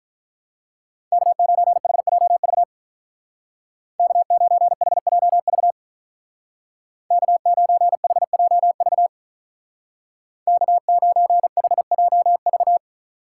Morse code, heard as a keyed tone, about 43 wpm, 700 Hz. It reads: K9HJV K9HJV K9HJV K9HJV
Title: Speed Racing - USA 1x3 Call Signs 35wpm